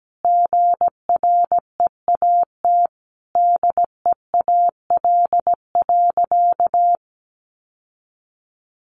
Morse code, heard as a keyed tone, about 17 wpm, 700 Hz. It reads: GREAT DEAL.
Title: Sentences from Top 1500 Words 17wpm